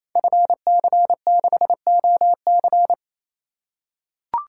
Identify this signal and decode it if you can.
Morse code, heard as a keyed tone, about 27 wpm, 700 Hz. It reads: FC6OC
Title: Code Groups - 5 Letter:Numbers - 28wpm